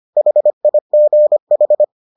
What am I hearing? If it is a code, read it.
Morse code, heard as a keyed tone, about 25 wpm, 600 Hz. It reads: HIGH